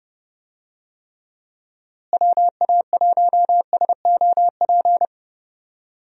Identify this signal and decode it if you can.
Morse code, heard as a keyed tone, about 30 wpm, 700 Hz. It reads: WA1SOP
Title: USA Call Signs - 6 Characters - Repeated 2x - 30wpm